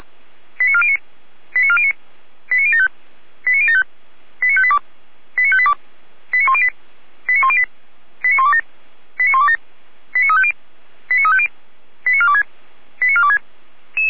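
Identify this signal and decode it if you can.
Five-tone selective call (ZVEI), heard as quick runs of five tones, tones of about 70 ms. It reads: 89409 89409 89075 89075 89582 89582 89108 89108 89127 89127 89350 89350 89426 89426
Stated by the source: ZVEI five-tone selective call 89018 received by SDR.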